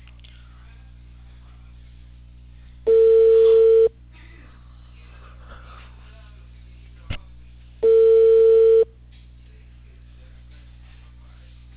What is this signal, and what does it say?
Ringback tone (China, 450 Hz).